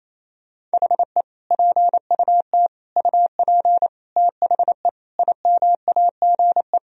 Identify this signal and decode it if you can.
Morse code, heard as a keyed tone, about 27 wpm, 700 Hz. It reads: HE PUT UP THE IMAGE
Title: Sentences from Top 1000 Words - Review Part 4 - 28wpm